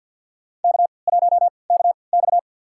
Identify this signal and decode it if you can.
Morse code, heard as a keyed tone, about 51 wpm, spaced as for 28 wpm, 700 Hz. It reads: K1KX
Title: USA Call Signs - 4 Characters - 3x Character Spacing - 50wpm